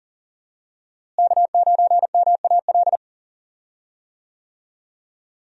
Morse code, heard as a keyed tone, about 39 wpm, 700 Hz. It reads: K9MAL
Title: USA Call Signs - 5 Characters - Repeated 3x - 40wpm